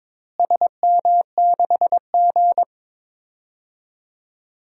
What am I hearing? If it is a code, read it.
Morse code, heard as a keyed tone, about 22 wpm, 700 Hz. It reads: SM6G
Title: DXCC-100 Call Signs - 4 Characters - Repeated 2x - 22wpm